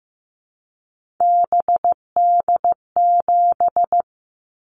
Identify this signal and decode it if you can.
Morse code, heard as a keyed tone, about 15 wpm, 700 Hz. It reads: BD7
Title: Callsign Prefixes for Top 100 DXCC - 15wpm